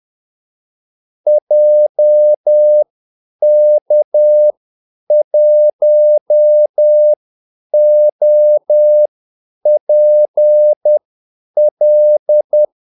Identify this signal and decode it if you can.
Morse code, heard as a keyed tone, about 10 wpm, spaced as for 8 wpm, 600 Hz. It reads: JK1OPL
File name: Callsigns-10-8wpm